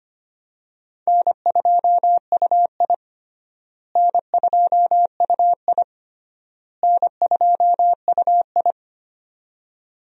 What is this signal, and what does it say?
Morse code, heard as a keyed tone, about 25 wpm, 700 Hz. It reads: N2UI N2UI N2UI
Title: USA Call Signs - 4 Characters - Repeated 3x - 25wpm